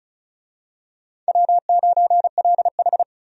Morse code, heard as a keyed tone, about 36 wpm, 700 Hz. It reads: W9LH
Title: USA Call Signs - 4 Characters - 35wpm